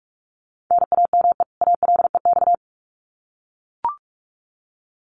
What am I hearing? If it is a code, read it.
Morse code, heard as a keyed tone, about 44 wpm, 700 Hz. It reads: NAME ALEX
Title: Common QSO Phrases 45wpm